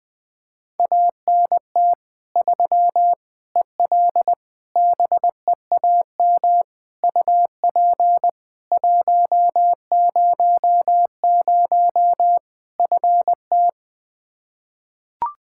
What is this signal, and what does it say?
Morse code, heard as a keyed tone, about 20 wpm, 700 Hz. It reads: ANT 3 EL BEAM UP 100 FT
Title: Practice QSO - Exchange 2 - Info - 20wpm